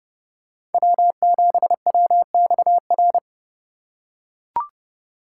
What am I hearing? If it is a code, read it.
Morse code, heard as a keyed tone, about 29 wpm, 700 Hz. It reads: W7WXR